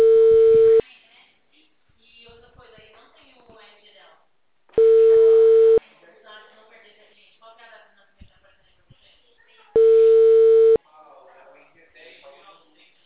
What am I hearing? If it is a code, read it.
Ringback tone (China, 450 Hz), with speech.
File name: out- -2014- - - .82764